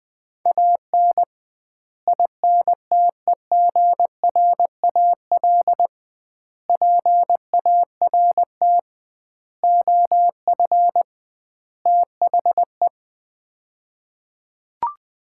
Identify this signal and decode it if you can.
Morse code, heard as a keyed tone, about 20 wpm, 700 Hz. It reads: AN INTEGRAL PART OF THE
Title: 2x Word-Spacing - Sets of 5 Words - 20wpm